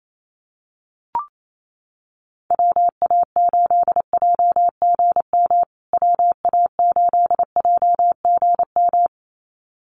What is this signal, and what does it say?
Morse code, heard as a keyed tone, about 28 wpm, 700 Hz. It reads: WA8JGM WA8JGM
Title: USA Call Signs Repeated 2x 28wpm